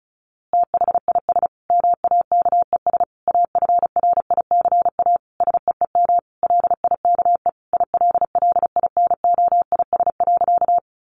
Morse code, heard as a keyed tone, about 34 wpm, 700 Hz. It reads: THIS MAKES AFRICA SEEM LIKE ILLINOIS.
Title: A Sound Of Thunder 35wpm